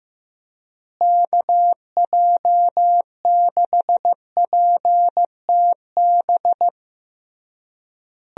Morse code, heard as a keyed tone, about 15 wpm, 700 Hz. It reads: KJ6PTB
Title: USA Call Signs - 6 Characters - Repeated 3x - 15wpm